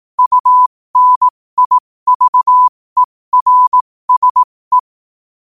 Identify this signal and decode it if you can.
Morse code, heard as a keyed tone, about 18 wpm, spaced as for 15 wpm, 1000 Hz. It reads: UNIVERSE
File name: qotdfiles / 15wpm